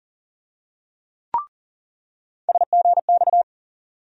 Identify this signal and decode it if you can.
Morse code, heard as a keyed tone, about 40 wpm, 700 Hz. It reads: SGX
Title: USA Call Sign Postfixes - 3 Letters 40wpm